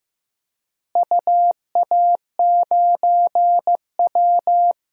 Morse code, heard as a keyed tone, about 15 wpm, 700 Hz. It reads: UA9W
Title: DXCC-100 Call Signs - 4 Characters - Repeated 2x - 15wpm